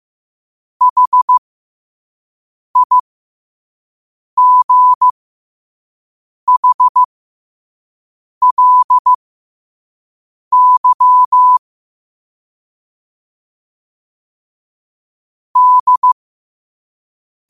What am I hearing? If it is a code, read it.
Morse code, heard as a keyed tone, about 15 wpm, spaced as for 5 wpm, 1000 Hz. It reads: HIGHLY D
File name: qotdfiles / 5wpm